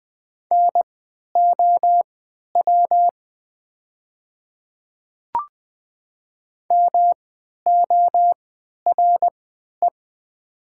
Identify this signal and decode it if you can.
Morse code, heard as a keyed tone, about 20 wpm, spaced as for 11 wpm, 700 Hz. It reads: NOW MORE
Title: 3x Character Spacing - Top 200 Words - 20wpm